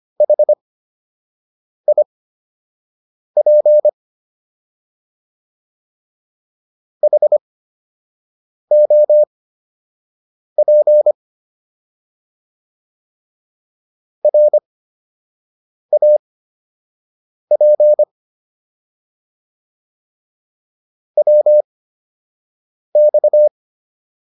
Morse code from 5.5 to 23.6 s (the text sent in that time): HOP RAP WX